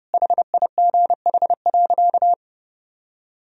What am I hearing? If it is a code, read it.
Morse code, heard as a keyed tone, about 29 wpm, 700 Hz. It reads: HIGH.
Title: Sentences from Top 1000 Words - Review Part 1 - 30wpm